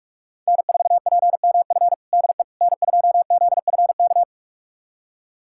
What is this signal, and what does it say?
Morse code, heard as a keyed tone, about 46 wpm, 700 Hz. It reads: N4PMF DE N2ZFK
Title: USA Callsign DE Callsign - 5 Characters - 45wpm